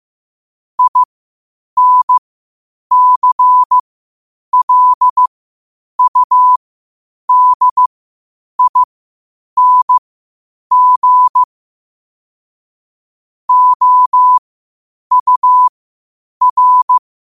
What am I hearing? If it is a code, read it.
Morse code, heard as a keyed tone, about 15 wpm, spaced as for 8 wpm, 1000 Hz. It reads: INCLUDING OUR